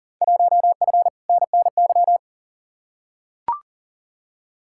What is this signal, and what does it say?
Morse code, heard as a keyed tone, about 40 wpm, 700 Hz. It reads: 1F NNY
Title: Field Day Exchanges - 40wpm